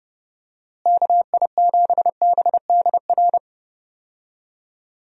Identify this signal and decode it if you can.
Morse code, heard as a keyed tone, about 30 wpm, 700 Hz. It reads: KI7BDR
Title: USA Call Signs - 6 Characters - Repeated 3x - 30wpm